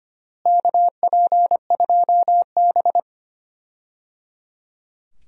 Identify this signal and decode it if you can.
Morse code, heard as a keyed tone, about 25 wpm, 700 Hz. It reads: KP2B
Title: DXCC-100 Call Signs - 4 Characters - Repeated 3x - 25wpm